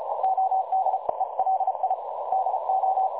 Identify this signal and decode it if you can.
Morse code, heard as a keyed tone, about 35 wpm, 760 Hz, with noise. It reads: OD ELI DO
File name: dlazaawansowanych / nagraniazpasma / websdr recording 3565.0kHz